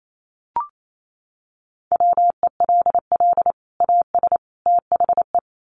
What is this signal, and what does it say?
Morse code, heard as a keyed tone, about 28 wpm, 700 Hz. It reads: WELL AS THE
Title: Sets of 3 Words 28wpm